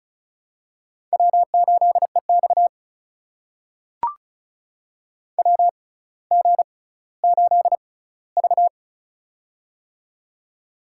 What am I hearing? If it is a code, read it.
Morse code, heard as a keyed tone, about 33 wpm, spaced as for 18 wpm, 700 Hz. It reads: W8EX WG8V
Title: USA Call Signs - 4 Characters - 6x Character Spacing - 35wpm